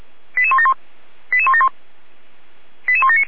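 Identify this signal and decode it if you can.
Five-tone selective call (ZVEI), heard as quick runs of five tones, tones of about 80 ms. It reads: 88171 88171 88179
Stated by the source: ZVEI five-tone selective call received on an SDR, archived by the Feuerwehr Orlamünde fire brigade.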